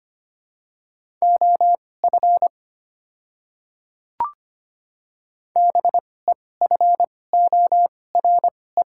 Morse code, heard as a keyed tone, about 25 wpm, spaced as for 18 wpm, 700 Hz. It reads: OF BEFORE
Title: 2x Character Spacing - Top 200 Words - 25wpm